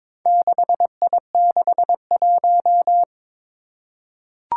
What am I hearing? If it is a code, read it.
Morse code, heard as a keyed tone, about 22 wpm, 700 Hz. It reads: 6I61
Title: Four Challenging Letter-Number Combinations 22wpm